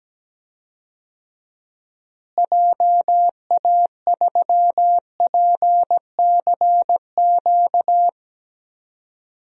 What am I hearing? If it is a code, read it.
Morse code, heard as a keyed tone, about 17 wpm, 700 Hz. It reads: JA3PCQ
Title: DXCC-100 Call Signs - 6 Characters - Repeated 3x - 17wpm